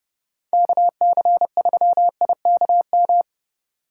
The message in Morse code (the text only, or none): KC3IKM